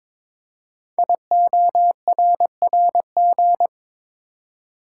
Morse code, heard as a keyed tone, about 22 wpm, 700 Hz. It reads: IORRG
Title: Code Groups - 5 Letters - 22wpm